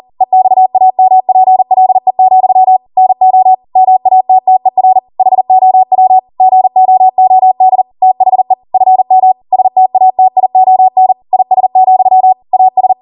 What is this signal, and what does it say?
Morse code, heard as a keyed tone, about 40 wpm, 750 Hz. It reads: EXAMPLE, NO MATTER HOW GOOD THE FM STATION IS, AS
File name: Morse / 40WPM